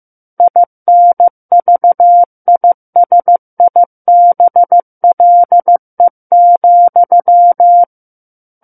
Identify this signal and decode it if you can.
Morse code, heard as a keyed tone, about 15 wpm, 700 Hz. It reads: INVISIBLE,
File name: Chapter 6 - 15 WPM